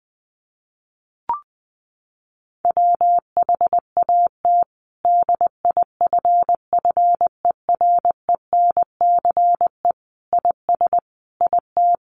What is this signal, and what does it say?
Morse code, heard as a keyed tone, about 20 wpm, 700 Hz. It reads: WHAT DIFFERENCE IS IT
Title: Sentences from Top 800 Words 20wpm